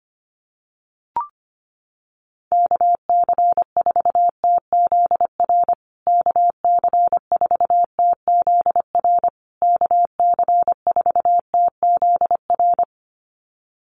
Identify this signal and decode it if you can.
Morse code, heard as a keyed tone, about 25 wpm, 700 Hz. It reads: KC4TZR KC4TZR KC4TZR
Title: USA Call Signs Repeated 3x 25wpm